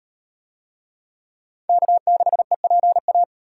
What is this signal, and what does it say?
Morse code, heard as a keyed tone, about 34 wpm, 700 Hz. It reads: K6EPA